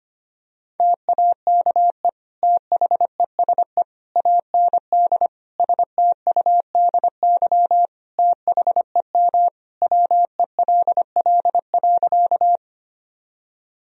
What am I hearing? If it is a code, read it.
Morse code, heard as a keyed tone, about 25 wpm, 700 Hz. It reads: TAKE THESE AND STUDY THEM WELL.
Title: Sentences from Top 1000 Words - Review Part 1 - 25wpm